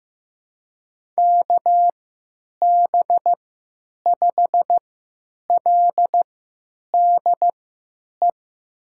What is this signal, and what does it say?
Morse code, heard as a keyed tone, about 15 wpm, spaced as for 9 wpm, 700 Hz. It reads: KB5LDE
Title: USA Call Signs - 6 Characters - 3x Character Spacing - 15wpm